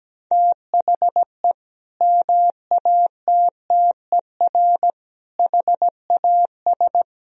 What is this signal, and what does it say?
Morse code, heard as a keyed tone, about 17 wpm, 700 Hz. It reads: THE MATTER HAS
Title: Sentences from Top 1000 Words - Review Part 3 - 17wpm